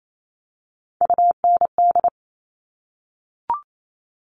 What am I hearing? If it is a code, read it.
Morse code, heard as a keyed tone, about 28 wpm, 700 Hz. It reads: UND
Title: Top Three Letter Combinations in Words 28wpm